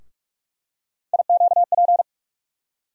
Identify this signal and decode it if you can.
Morse code, heard as a keyed tone, about 46 wpm, 700 Hz. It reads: IQP
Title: USA Call Sign Postfixes - 3 Letters 45wpm